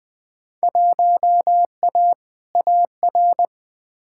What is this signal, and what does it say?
Morse code, heard as a keyed tone, about 20 wpm, 700 Hz. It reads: 1A AR
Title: Field Day Exchanges Encore - 20wpm